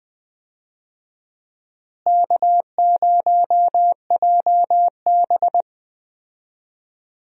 Morse code, heard as a keyed tone, about 20 wpm, 700 Hz. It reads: K0JB
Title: USA Call Signs - 4 Characters - Repeated 3x - 20wpm